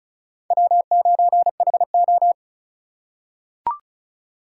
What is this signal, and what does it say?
Morse code, heard as a keyed tone, about 36 wpm, 700 Hz. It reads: W9HO
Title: USA Call Signs - 4 Characters - 4x Character Spacing - 35wpm